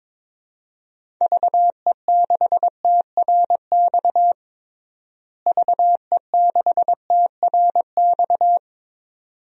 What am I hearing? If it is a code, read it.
Morse code, heard as a keyed tone, about 22 wpm, 700 Hz. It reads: VE6TRX VE6TRX
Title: DXCC-100 Call Signs - 6 Characters - Repeated 2x - 22wpm